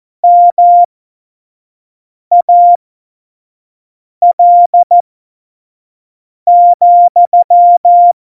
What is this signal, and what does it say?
Morse code, heard as a keyed tone, about 14 wpm, spaced as for 5 wpm, 700 Hz. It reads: MAL,
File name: Morsekurs / Witze / Witz 05. September 17, Tempo 25